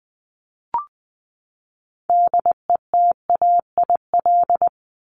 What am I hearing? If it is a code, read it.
Morse code, heard as a keyed tone, about 20 wpm, 700 Hz. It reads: DETAIL
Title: Top 1251-1500 Words 20wpm